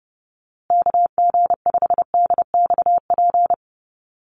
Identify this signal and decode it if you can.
Morse code, heard as a keyed tone, about 30 wpm, 700 Hz. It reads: KG5DXP
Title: USA Call Signs Repeated 3x 30wpm